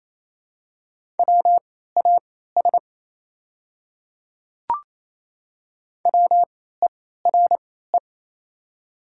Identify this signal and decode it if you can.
Morse code, heard as a keyed tone, about 28 wpm, spaced as for 16 wpm, 700 Hz. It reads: WAS WERE
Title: 3x Character Spacing - Top 100 Words - 28wpm